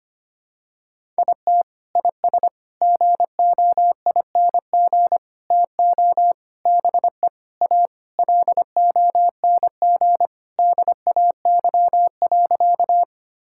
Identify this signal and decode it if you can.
Morse code, heard as a keyed tone, about 25 wpm, 700 Hz. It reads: IT IS GOING TO BE A LONG DAY.
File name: practice / 25wpm / Sentences with 4-Word Phrases - 25wpm